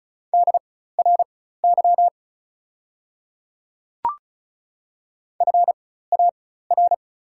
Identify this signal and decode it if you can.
Morse code, heard as a keyed tone, about 36 wpm, spaced as for 16 wpm, 700 Hz. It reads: DRY FAR
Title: 4x Character Spacing - Top 500 Words - 35wpm